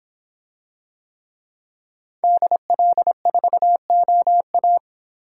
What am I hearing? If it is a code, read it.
Morse code, heard as a keyed tone, about 26 wpm, 700 Hz. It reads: DL4OA